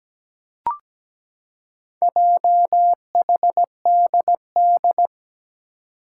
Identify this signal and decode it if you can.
Morse code, heard as a keyed tone, about 17 wpm, 700 Hz. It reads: JHDD
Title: Four Challenging Letter-Number Combinations 17wpm